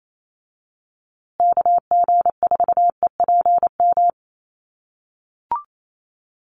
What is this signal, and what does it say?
Morse code, heard as a keyed tone, about 28 wpm, 700 Hz. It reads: KG4EPM
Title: USA Call Signs Repeated 2x 28wpm